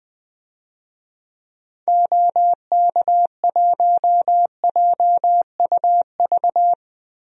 Morse code, heard as a keyed tone, about 20 wpm, 700 Hz. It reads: OK1JUV
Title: DXCC-100 Call Signs - 6 Characters - 20wpm